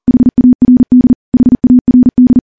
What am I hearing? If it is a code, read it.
Morse code, heard as a keyed tone, about 40 wpm, 250 Hz. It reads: HARD HARD
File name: morse / morse-training-40WPM